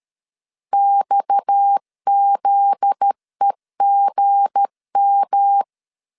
Morse code, heard as a keyed tone, about 13 wpm, 800 Hz. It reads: XZEGM
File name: LEZIONI / 8 YJ45 / 8 YJ 12wpm 2235